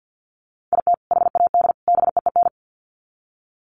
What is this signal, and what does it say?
Morse code, heard as a keyed tone, about 50 wpm, 700 Hz. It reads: IT HAD BEEN
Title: Sets of 3 Words 50wpm